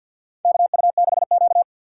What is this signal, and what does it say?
Morse code, heard as a keyed tone, about 51 wpm, 700 Hz. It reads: KU6Q
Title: USA Call Signs - 4 Characters - 6x Character Spacing - 50wpm